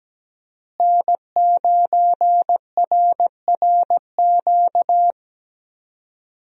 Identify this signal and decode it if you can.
Morse code, heard as a keyed tone, about 17 wpm, 700 Hz. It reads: N9RRQ
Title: USA Call Signs - 5 Characters - Repeated 2x - 17wpm